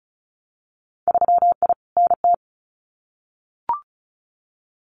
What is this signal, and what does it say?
Morse code, heard as a keyed tone, about 34 wpm, 700 Hz. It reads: 3I NT
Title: Winter Field Day Exchanges Encore 35wpm